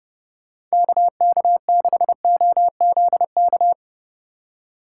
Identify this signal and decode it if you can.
Morse code, heard as a keyed tone, about 36 wpm, 700 Hz. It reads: KK6OZK
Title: Speed Racing - USA 2x3 Call Signs 30wpm